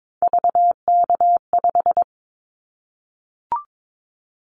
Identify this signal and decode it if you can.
Morse code, heard as a keyed tone, about 22 wpm, 700 Hz. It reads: VK5
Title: Callsign Prefixes for Top 50 DXCC - 22wpm